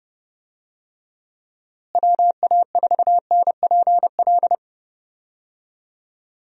Morse code, heard as a keyed tone, about 30 wpm, 700 Hz. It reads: WA4NPL
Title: USA Call Signs - 6 Characters - Repeated 3x - 30wpm